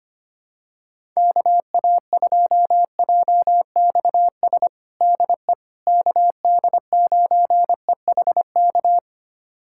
Morse code, heard as a keyed tone, about 25 wpm, 700 Hz. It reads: KA2JXS DE KD9EHK
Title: USA Callsign DE Callsign - 6 Characters - 25wpm